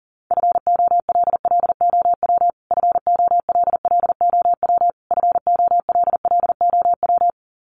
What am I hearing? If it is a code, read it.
Morse code, heard as a keyed tone, about 40 wpm, 700 Hz. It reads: FOLLOW FOLLOW FOLLOW
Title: Top 500 Words Repeated 40wpm